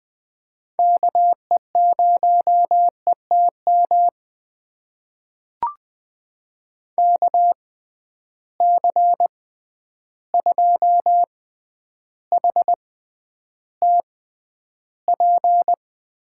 Morse code, heard as a keyed tone, about 19 wpm, spaced as for 10 wpm, 700 Hz. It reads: KE0ETM KC2HTP
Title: USA Call Signs - 6 Characters - 6x Character Spacing - 20wpm